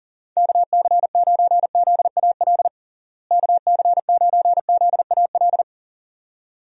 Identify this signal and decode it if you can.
Morse code, heard as a keyed tone, about 40 wpm, 700 Hz. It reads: KC9ZAL KC9ZAL
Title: USA Call Signs - 6 Characters - Repeated 2x - 40wpm